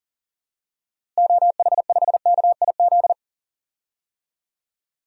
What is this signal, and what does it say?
Morse code, heard as a keyed tone, about 38 wpm, 700 Hz. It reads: OH5KIZ
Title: DXCC-100 Call Signs - 6 Characters - Repeated 3x - 40wpm